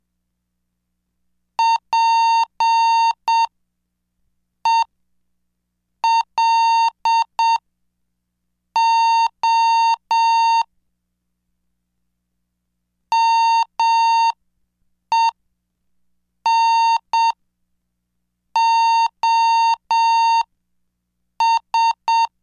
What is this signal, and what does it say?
Morse code, heard as a keyed tone, about 7 wpm, spaced as for 5 wpm, 920 Hz. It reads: PELO MENOS
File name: cursos / CDestudo / 26 Track 26